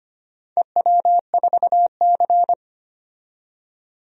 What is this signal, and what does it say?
Morse code, heard as a keyed tone, about 25 wpm, 700 Hz. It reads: EW4C
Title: DXCC-100 Call Signs - 4 Characters - 25wpm